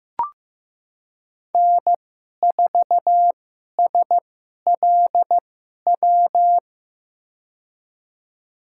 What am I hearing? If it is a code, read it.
Morse code, heard as a keyed tone, about 15 wpm, spaced as for 11 wpm, 700 Hz. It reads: N4SLW
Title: USA Call Signs - 5 Characters - 2x Character Spacing - 15wpm